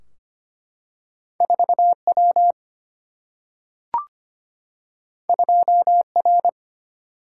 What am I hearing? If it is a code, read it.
Morse code, heard as a keyed tone, about 25 wpm, 700 Hz. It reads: 4W 2R
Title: Single-Letter Single-Number 25wpm